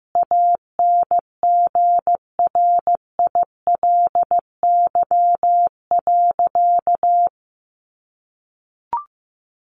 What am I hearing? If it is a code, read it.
Morse code, heard as a keyed tone, about 15 wpm, 700 Hz. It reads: ANGRILY.
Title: A Sound Of Thunder 15wpm